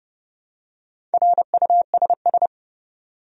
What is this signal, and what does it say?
Morse code, heard as a keyed tone, about 32 wpm, 700 Hz. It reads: RUSS
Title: USA Common Names - Max 4 Letters - 30wpm